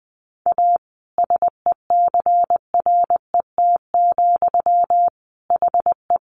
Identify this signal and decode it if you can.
Morse code, heard as a keyed tone, about 20 wpm, 700 Hz. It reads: A SECRET, HE
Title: Sentences from Top 1200 Words 20wpm